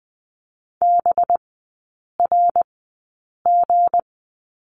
Morse code, heard as a keyed tone, about 20 wpm, spaced as for 8 wpm, 700 Hz. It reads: BRG
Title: 184 Lesson 30e - Letter b - Three Characters 20wpm